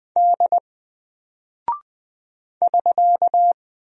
Morse code, heard as a keyed tone, about 20 wpm, 700 Hz. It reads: D <SK>
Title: Single Character (Letter:Number:Punctuation:Prosign) Alt 20wpm